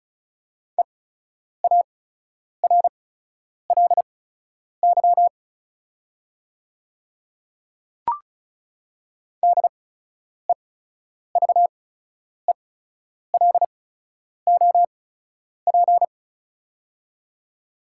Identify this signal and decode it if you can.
Morse code, heard as a keyed tone, about 36 wpm, spaced as for 10 wpm, 700 Hz. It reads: EARLY DEVELOP
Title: 8x Character Spacing - Top 500 Words - 35wpm